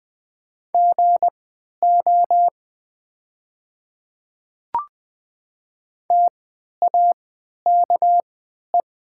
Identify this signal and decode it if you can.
Morse code, heard as a keyed tone, about 20 wpm, spaced as for 11 wpm, 700 Hz. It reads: GO TAKE
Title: 3x Character Spacing - Top 300 Words - 20wpm